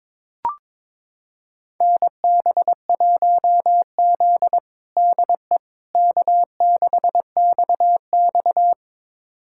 Morse code, heard as a keyed tone, about 21 wpm, 700 Hz. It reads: NB1Z DE K6XX
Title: USA Callsign DE Callsign - 4 Characters - 22wpm